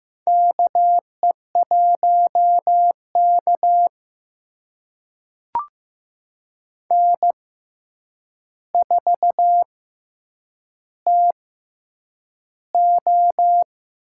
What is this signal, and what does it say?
Morse code, heard as a keyed tone, about 15 wpm, spaced as for 8 wpm, 700 Hz. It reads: KE1K N4TO
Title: USA Call Signs - 4 Characters - 6x Character Spacing - 15wpm